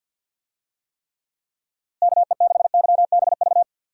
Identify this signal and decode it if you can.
Morse code, heard as a keyed tone, about 49 wpm, 700 Hz. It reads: KE6YBV
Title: USA Call Signs - 6 Characters - Repeated 3x - 50wpm